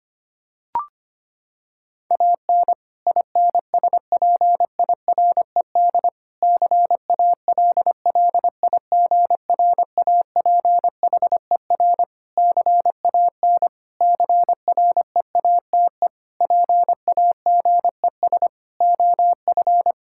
Morse code, heard as a keyed tone, about 25 wpm, 700 Hz. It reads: AN INSPIRED CALLIGRAPHER CAN CREATE PAGES OF
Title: Pangram Challenge v2 - 25wpm